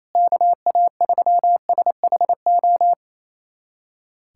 Morse code, heard as a keyed tone, about 28 wpm, 700 Hz. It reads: KA3SHO
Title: USA Callsign DE Callsign 28wpm